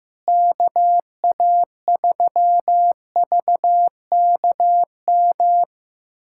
Morse code, heard as a keyed tone, about 15 wpm, 700 Hz. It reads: KA3VKM
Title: USA Callsign DE Callsign - 6 Characters - 15wpm